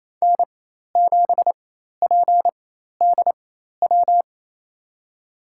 Morse code, heard as a keyed tone, about 28 wpm, spaced as for 13 wpm, 700 Hz. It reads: N7PDW
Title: USA Call Signs - 5 Characters - 4x Character Spacing - 28wpm